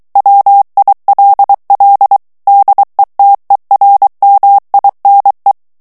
Morse code, heard as a keyed tone, about 23 wpm, 800 Hz. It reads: WILL DETERMINE
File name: ANGLAIS texte clair 3 vitesse 22 mots minute